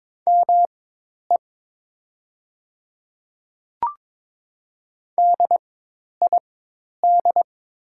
Morse code, heard as a keyed tone, about 22 wpm, spaced as for 10 wpm, 700 Hz. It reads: ME DID
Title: 4x Character Spacing - Top 200 Words - 22wpm